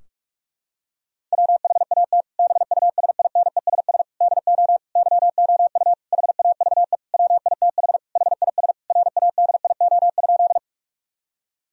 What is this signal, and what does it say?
Morse code, heard as a keyed tone, about 42 wpm, 700 Hz. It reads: WHAT BUSINESS DO YOU HAVE WITH HIS RADIO?
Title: Sentences from Top 1000 Words - Review Part 3 - 45wpm